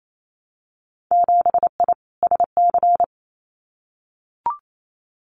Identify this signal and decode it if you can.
Morse code, heard as a keyed tone, about 28 wpm, 690 Hz. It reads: 7I SC
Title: Winter Field Day Exchanges Encore 28wpm